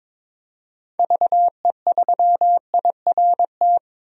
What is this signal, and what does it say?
Morse code, heard as a keyed tone, about 22 wpm, 700 Hz. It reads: VE3IRT
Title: DXCC-100 Call Signs - 6 Characters - 22wpm